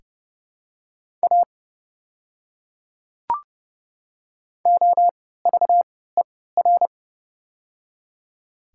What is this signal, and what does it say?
Morse code, heard as a keyed tone, about 31 wpm, spaced as for 17 wpm, 700 Hz. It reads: A OVER
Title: 3x Character Spacing - Top 100 Words - 30wpm